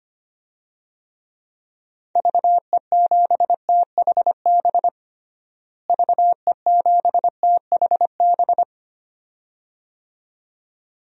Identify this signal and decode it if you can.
Morse code, heard as a keyed tone, about 25 wpm, 700 Hz. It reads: VE7THB VE7THB
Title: DXCC-100 Call Signs - 6 Characters - Repeated 2x - 25wpm